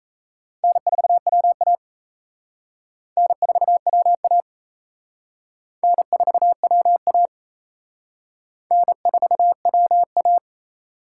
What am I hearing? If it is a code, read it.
Morse code, heard as a keyed tone, about 34 wpm, 700 Hz. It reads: N4WA N4WA N4WA N4WA
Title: Speed Racing - USA 1x2 Call Signs 28wpm